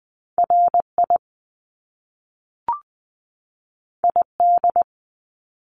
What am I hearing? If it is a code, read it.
Morse code, heard as a keyed tone, about 20 wpm, 700 Hz. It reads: R I I D
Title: Top Two Letter Combinations in Words 20wpm